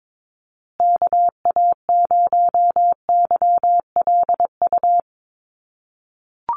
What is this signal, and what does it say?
Morse code, heard as a keyed tone, about 22 wpm, 700 Hz. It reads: KA0YLU
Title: USA Call Signs Repeated 2x 22wpm